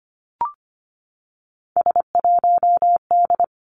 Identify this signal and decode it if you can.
Morse code, heard as a keyed tone, about 25 wpm, 700 Hz. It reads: S1D
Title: Three Challenging Letter-Number Combinations 25wpm